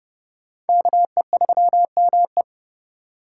Morse code, heard as a keyed tone, about 29 wpm, 700 Hz. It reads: KE3ME